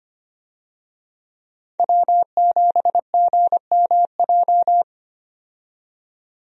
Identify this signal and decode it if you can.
Morse code, heard as a keyed tone, about 25 wpm, 700 Hz. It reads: W7GMJ
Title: USA Call Signs - 5 Characters - 25wpm